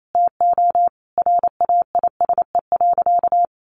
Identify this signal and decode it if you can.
Morse code, heard as a keyed tone, about 28 wpm, 700 Hz. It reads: TO RAISE.
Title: Sentences from Top 1400 Words 28wpm